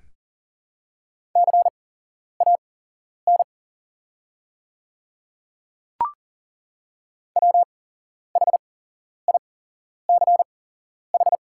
Morse code, heard as a keyed tone, about 40 wpm, spaced as for 11 wpm, 690 Hz. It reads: CAN WHICH
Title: 8x Character Spacing - Top 100 Words - 40wpm